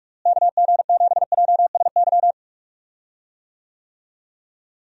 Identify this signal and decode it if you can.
Morse code, heard as a keyed tone, about 46 wpm, 700 Hz. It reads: KG7JSY
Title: USA Call Signs - 6 Characters - Repeated 2x - 45wpm